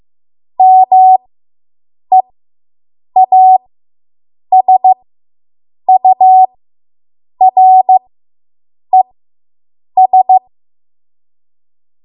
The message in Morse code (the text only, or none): MEASURES